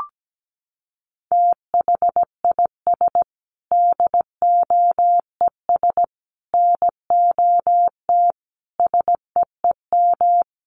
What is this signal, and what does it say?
Morse code, heard as a keyed tone, about 17 wpm, 700 Hz. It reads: THIS DOES NOT SEEM
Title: Sentences from Top 1400 Words 17wpm